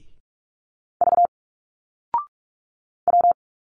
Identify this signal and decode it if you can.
Morse code, heard as a keyed tone, about 46 wpm, 700 Hz. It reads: V W
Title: Single Letters 45wpm